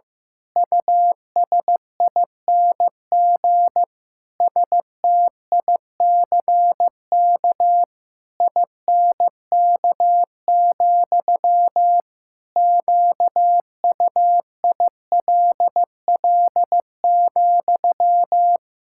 Morse code, heard as a keyed tone, about 15 wpm, 700 Hz. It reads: USING STICK INK, QUILL,
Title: Pangram Challenge v2 - 15wpm